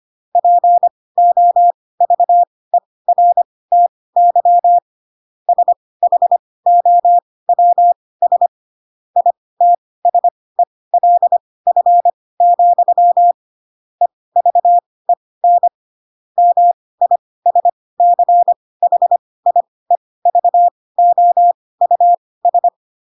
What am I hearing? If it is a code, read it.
Morse code, heard as a keyed tone, about 25 wpm, spaced as for 18 wpm, 700 Hz. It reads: POVERTY SHOWS ITSELF, EVEN MISCHIEVOUS